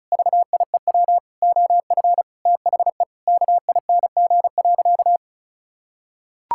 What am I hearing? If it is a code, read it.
Morse code, heard as a keyed tone, about 36 wpm, 700 Hz. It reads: VIEW OF THE KING.
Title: Sentences from Top 1000 Words - Review Part 6 - 35wpm